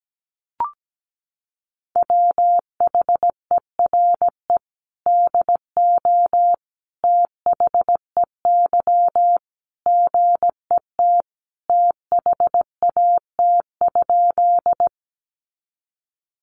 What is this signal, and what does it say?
Morse code, heard as a keyed tone, about 17 wpm, 700 Hz. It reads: WHERE DO THEY GET THAT?
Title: Sentences from Top 200 Words 17wpm